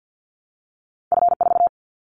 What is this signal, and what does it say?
Morse code, heard as a keyed tone, about 50 wpm, 700 Hz. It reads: F4
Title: Callsign Prefixes for Top 25 DXCC - Repeated 3x - 50wpm